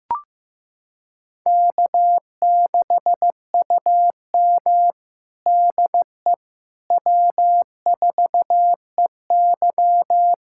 Morse code, heard as a keyed tone, about 16 wpm, 700 Hz. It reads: K6UM DE W4EY
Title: USA Callsign DE Callsign - 4 Characters - 15wpm